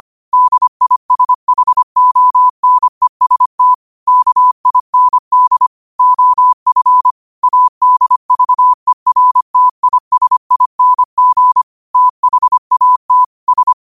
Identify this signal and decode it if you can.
Morse code, heard as a keyed tone, about 25 wpm, 1000 Hz. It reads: DISHONEST KIND OF ADVERTISING THAT S
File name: qotdfiles / 25wpm